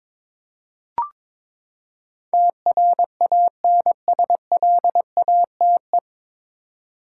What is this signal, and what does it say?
Morse code, heard as a keyed tone, about 21 wpm, 700 Hz. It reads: TRANSLATE
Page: Morse Code Ninja practice 22wpm